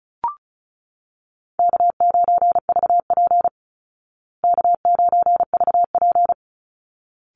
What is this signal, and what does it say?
Morse code, heard as a keyed tone, about 34 wpm, 700 Hz. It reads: K9VP K9VP
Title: Easy Call Signs Repeated 2x - 35wpm